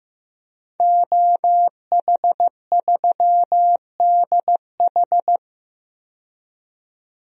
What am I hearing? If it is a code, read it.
Morse code, heard as a keyed tone, about 15 wpm, 700 Hz. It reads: OH3DH
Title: DXCC-100 Call Signs - 5 Characters - 15wpm